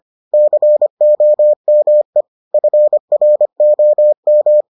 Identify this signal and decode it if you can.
Morse code, heard as a keyed tone, about 25 wpm, 600 Hz. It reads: COME FROM